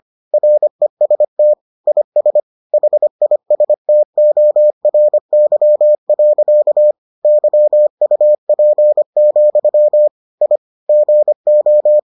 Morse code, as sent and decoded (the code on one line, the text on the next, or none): .-. ... - / .. ... / .... .. ... - --- .-. -.-- .-.-.- / -.-- ..- .--. --..-- / .. / --. ---
REST IS HISTORY. YUP, I GO